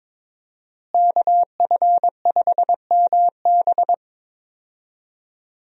Morse code, heard as a keyed tone, about 22 wpm, 700 Hz. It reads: KF5MB
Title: USA Call Signs - 5 Characters - Repeated 2x - 22wpm